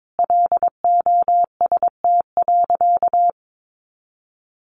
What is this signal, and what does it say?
Morse code, heard as a keyed tone, about 22 wpm, 700 Hz. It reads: LOST.
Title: Sentences from Top 1400 Words 22wpm